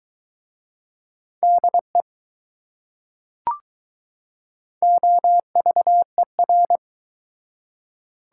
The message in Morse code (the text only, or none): DE OVER